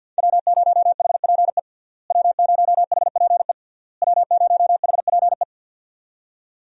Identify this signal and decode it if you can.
Morse code, heard as a keyed tone, about 51 wpm, 700 Hz. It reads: W0HPE W0HPE W0HPE
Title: USA Call Signs - 5 Characters - Repeated 3x - 50wpm